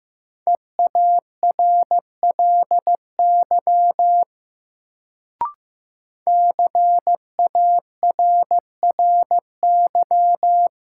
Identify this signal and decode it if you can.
Morse code, heard as a keyed tone, about 15 wpm, 700 Hz. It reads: EARLY CARRY
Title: Sentences from Top 400 Words - New Word Drill - 15wpm